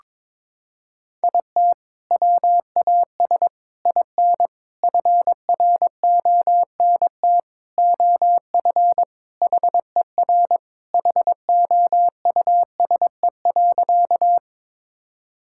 Morse code, heard as a keyed tone, about 22 wpm, 700 Hz. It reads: IT WAS IN FRONT OF HER HOUSE.
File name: practice / 22wpm / Sentences with 4-Word Phrases - 22wpm